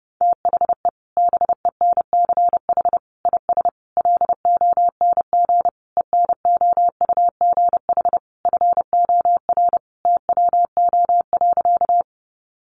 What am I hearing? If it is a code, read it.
Morse code, heard as a keyed tone, about 30 wpm, 700 Hz. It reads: THE BENCH IS LONG ENOUGH FOR TWO.
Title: Sentences from Top 1400 Words 30wpm